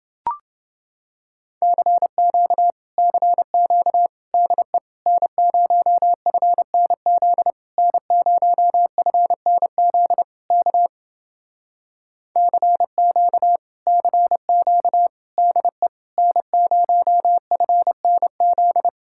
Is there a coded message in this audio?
Yes: Morse code from 1.6 to 18.9 s, heard as a keyed tone, about 30 wpm, 700 Hz. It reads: CQ CQ DE N0FNZ N0FNZ K CQ CQ DE N0FNZ